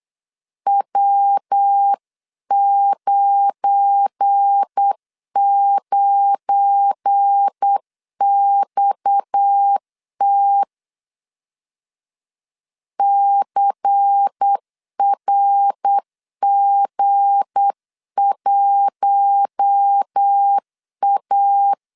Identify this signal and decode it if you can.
Morse code, heard as a keyed tone, about 8 wpm, 800 Hz. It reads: W99XT CRG1A
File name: LEZIONI / 7 GF9Z / 7 GF9 8wpm 3545